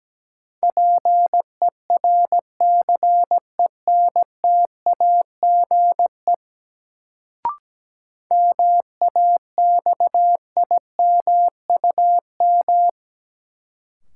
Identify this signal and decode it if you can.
Morse code, heard as a keyed tone, about 17 wpm, 700 Hz. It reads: PERCENTAGE MAXIMUM
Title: Sentences from Top 1300 Words - New Word Drill - 17wpm